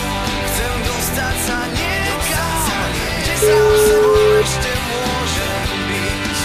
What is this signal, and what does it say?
Ringback tone (Europe, 425 Hz), with noise.